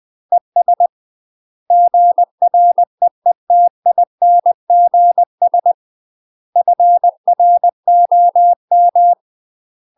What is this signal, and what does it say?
Morse code, heard as a keyed tone, about 20 wpm, 700 Hz. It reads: ES GREETINGS FROM
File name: UK MASTER QSOs TRAINING / 20WPM / UK MASTER QSOs 0005